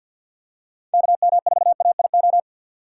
Morse code, heard as a keyed tone, about 53 wpm, 700 Hz. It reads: KM4AIO
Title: USA Call Signs - 6 Characters - 4x Character Spacing - 50wpm